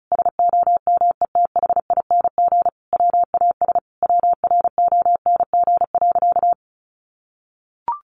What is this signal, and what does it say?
Morse code, heard as a keyed tone, about 34 wpm, 700 Hz. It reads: SOMETHING WAS WRONG.
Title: Sentences from Top 1200 Words 35wpm